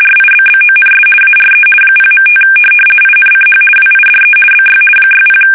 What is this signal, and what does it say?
RTTY, about 50 baud. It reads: RIBBONS ARE 0.5 INCH WIDE BY 60
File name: fsk850-50B / teletype-corp